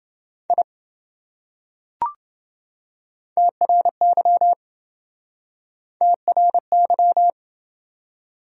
Morse code, heard as a keyed tone, about 26 wpm, 700 Hz. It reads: I TRY TRY